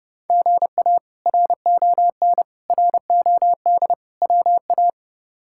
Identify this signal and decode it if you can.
Morse code, heard as a keyed tone, about 30 wpm, 700 Hz. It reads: GA RON ROD WA
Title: K1USN SST QSO Practice - 30wpm